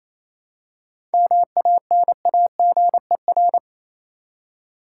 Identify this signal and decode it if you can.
Morse code, heard as a keyed tone, about 27 wpm, 700 Hz. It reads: MANAGER
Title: Seven Letter Words - 28wpm